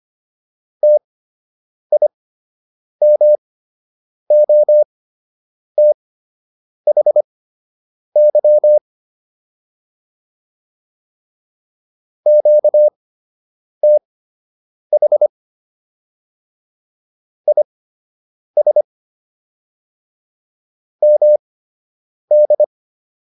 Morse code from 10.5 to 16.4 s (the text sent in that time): QTH